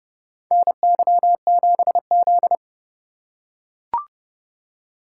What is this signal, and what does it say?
Morse code, heard as a keyed tone, about 30 wpm, 700 Hz. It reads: NY7Z
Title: K1USN SST QSO Practice - 30wpm